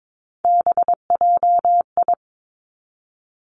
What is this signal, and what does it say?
Morse code, heard as a keyed tone, about 22 wpm, 700 Hz. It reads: BJI
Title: Three Challenging Letter-Number Combinations 22wpm